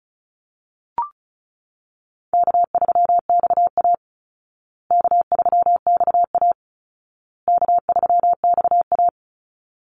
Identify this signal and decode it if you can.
Morse code, heard as a keyed tone, about 34 wpm, 700 Hz. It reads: K3XA K3XA K3XA
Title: Call Signs Repeated 2x - 35wpm